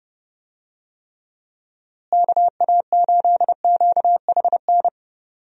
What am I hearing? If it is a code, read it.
Morse code, heard as a keyed tone, about 31 wpm, 700 Hz. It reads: KA8QHN
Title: USA Call Signs - 6 Characters - Repeated 3x - 30wpm